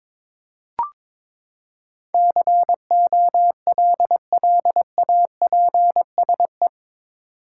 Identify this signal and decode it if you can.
Morse code, heard as a keyed tone, about 21 wpm, 700 Hz. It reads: COLLAPSE
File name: practice / 22wpm / Eight Letter Words - 22wpm